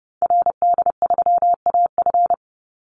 Morse code, heard as a keyed tone, about 30 wpm, 700 Hz. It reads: RD3AF
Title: Call Signs Repeated 2x - 30wpm